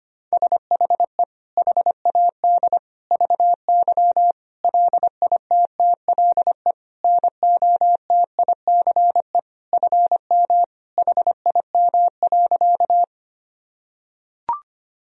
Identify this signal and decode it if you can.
Morse code, heard as a keyed tone, about 25 wpm, 700 Hz. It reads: SHE HAD VY LITTLE NOTICE FM HIM.
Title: Sentences from Top 400 Words and Abbreviations 25wpm